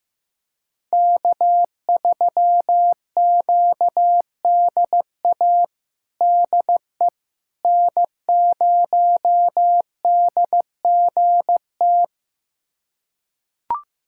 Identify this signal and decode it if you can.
Morse code, heard as a keyed tone, about 15 wpm, 700 Hz. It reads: K3QDA DE N0DGT
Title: USA Callsign DE Callsign - 5 Characters - 15wpm